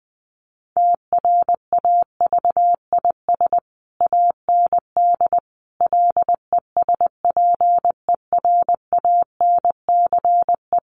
Morse code, heard as a keyed tone, about 20 wpm, 700 Hz. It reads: TRAVIS AND LESPERANCE
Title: A Sound Of Thunder 20wpm